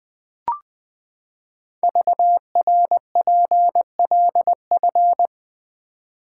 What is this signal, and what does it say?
Morse code, heard as a keyed tone, about 20 wpm, 700 Hz. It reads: VRPLF
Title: Code Groups - 5 Letters - 20wpm